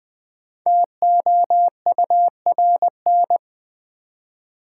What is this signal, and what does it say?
Morse code, heard as a keyed tone, about 20 wpm, 700 Hz. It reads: TOURN
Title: Build A Word - 5 Letters At A Time - 10-Letter Words - 20wpm